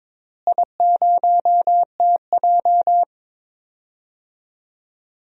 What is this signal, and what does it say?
Morse code, heard as a keyed tone, about 22 wpm, 700 Hz. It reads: I0TJ
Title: DXCC-100 Call Signs - 4 Characters - Repeated 3x - 22wpm